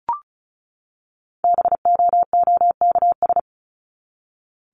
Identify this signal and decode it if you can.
Morse code, heard as a keyed tone, about 34 wpm, 700 Hz. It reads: BOOKS
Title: ES:S Suffix - Top 100 Words with Base Words 35wpm